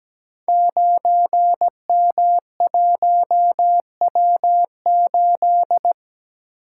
Morse code, heard as a keyed tone, about 17 wpm, 700 Hz. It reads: 9M1W8
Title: Code Groups - 5 Letter:Numbers - 17wpm